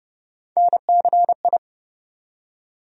Morse code, heard as a keyed tone, about 32 wpm, 700 Hz. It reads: NCI
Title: USA Call Sign Postfixes - 3 Letters 30wpm